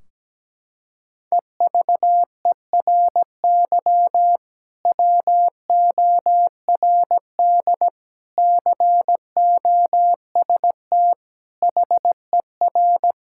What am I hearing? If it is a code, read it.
Morse code, heard as a keyed tone, about 17 wpm, 700 Hz. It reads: EVERY WORD COST HER
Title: Sentences from Top 1600 Words 17wpm